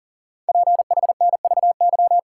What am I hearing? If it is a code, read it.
Morse code, heard as a keyed tone, about 44 wpm, 700 Hz. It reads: PHNVY
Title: Code Groups - 5 Letters - 40wpm